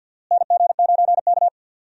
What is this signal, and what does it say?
Morse code, heard as a keyed tone, about 51 wpm, 700 Hz. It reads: NG9K
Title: USA Call Signs - 4 Characters - Repeated 2x - 50wpm